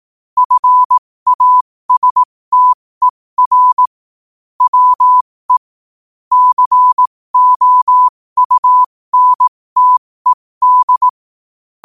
Morse code, heard as a keyed tone, about 18 wpm, spaced as for 15 wpm, 1000 Hz. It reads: FASTER WE COUNTED
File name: qotdfiles / 15wpm